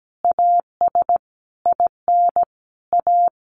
Morse code, heard as a keyed tone, about 17 wpm, 700 Hz. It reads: AS IN A
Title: Sentences from Top 1500 Words 17wpm